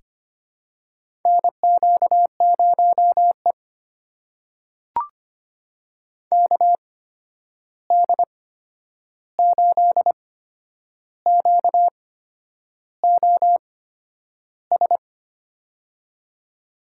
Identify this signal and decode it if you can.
Morse code, heard as a keyed tone, about 25 wpm, spaced as for 9 wpm, 700 Hz. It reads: NQ0E KD8QOS
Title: USA Call Signs - 8x Character Spacing - 25wpm